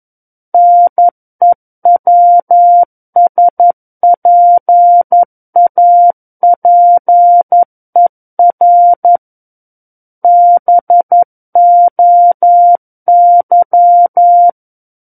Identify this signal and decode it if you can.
Morse code, heard as a keyed tone, about 11 wpm, 700 Hz. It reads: NEWSPAPER BOY
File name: 11 WPM